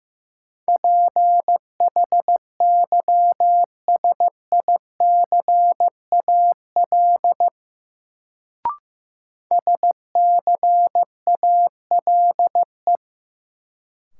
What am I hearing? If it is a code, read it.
Morse code, heard as a keyed tone, about 15 wpm, 700 Hz. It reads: PHYSICAL SCALE
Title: Sentences from Top 1000 Words - Review Part 4 - New Word Drill - 15wpm